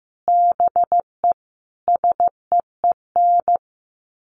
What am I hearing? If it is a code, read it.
Morse code, heard as a keyed tone, about 15 wpm, 700 Hz. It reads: BE SEEN
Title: Sets of 5 Words Encore 15wpm